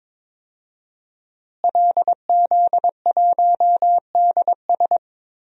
Morse code, heard as a keyed tone, about 22 wpm, 700 Hz. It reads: LZ1DS
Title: DXCC-100 Call Signs - 5 Characters - Repeated 2x - 22wpm